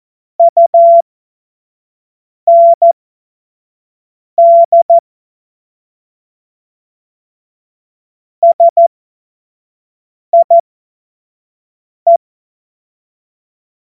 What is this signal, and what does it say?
Morse code, heard as a keyed tone, about 14 wpm, spaced as for 5 wpm, 680 Hz. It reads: UND SIE
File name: Morsekurs / Witze / Witz 09 April 2024 25 F70